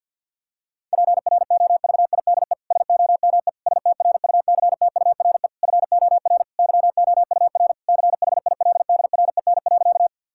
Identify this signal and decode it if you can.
Morse code, heard as a keyed tone, about 50 wpm, 700 Hz. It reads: PROVIDE SOME STRUCTURE FOR YOUR CHILDREN.
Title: Sentences from Top 1000 Words - Review Part 6 - 50wpm